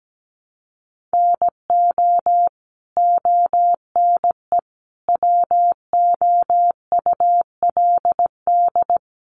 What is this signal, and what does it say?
Morse code, heard as a keyed tone, about 17 wpm, 700 Hz. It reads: NO ONE WOULD
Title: Sentences from Top 300 Words 17wpm